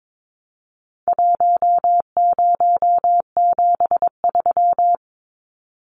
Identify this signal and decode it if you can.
Morse code, heard as a keyed tone, about 22 wpm, 700 Hz. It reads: 1073
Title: Numbers - 4-Digit 22wpm